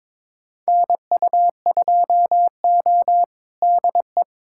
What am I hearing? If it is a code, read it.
Morse code, heard as a keyed tone, about 22 wpm, 700 Hz. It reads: NU2O DE
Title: USA Callsign DE Callsign - 4 Characters - 22wpm